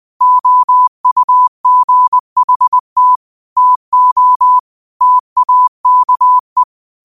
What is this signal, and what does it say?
Morse code, heard as a keyed tone, about 20 wpm, 1000 Hz. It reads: OUGHT TO TAKE
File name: qotdfiles / 20wpm